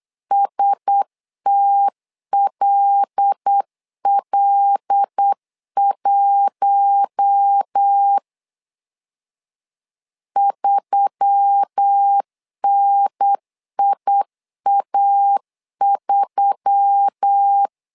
Morse code, heard as a keyed tone, about 8 wpm, 800 Hz. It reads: STLL1 3NIA3
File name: LEZIONI / 5 LBVC / 5 L 8wpm 2831